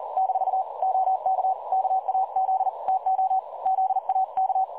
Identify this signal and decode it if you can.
Morse code, heard as a keyed tone, about 41 wpm, 760 Hz. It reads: - OK KAZ TO ZAK
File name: dlazaawansowanych / nagraniazpasma / websdr recording 3565.0kHz